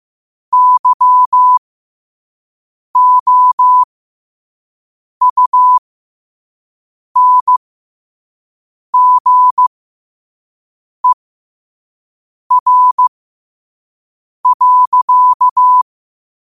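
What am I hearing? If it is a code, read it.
Morse code, heard as a keyed tone, about 15 wpm, spaced as for 5 wpm, 1000 Hz. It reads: YOUNGER.
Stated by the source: Morse code at 5 wpm.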